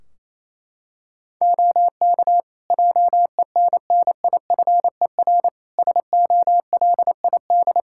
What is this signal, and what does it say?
Morse code, heard as a keyed tone, about 28 wpm, 700 Hz. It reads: OK JENNIFER SOLID
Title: Practice QSO - Exchange 3 - More Info - 28wpm